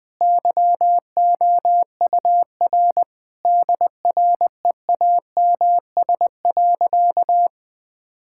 Morse code, heard as a keyed tone, about 20 wpm, 700 Hz. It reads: YOUR DREAMS.
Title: Difficult Sentences with Hints - 20wpm